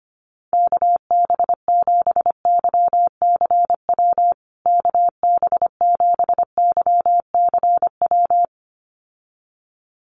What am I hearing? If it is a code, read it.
Morse code, heard as a keyed tone, about 25 wpm, 700 Hz. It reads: KB7YCW KB7YCW
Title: USA Call Signs Repeated 2x 25wpm